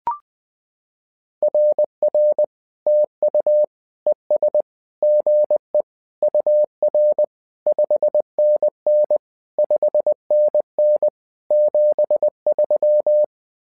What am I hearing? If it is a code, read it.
Morse code, heard as a keyed tone, about 20 wpm, 600 Hz. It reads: RR TU ES GE UR 5NN 5NN 73
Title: SOTA QSO Practice 20wpm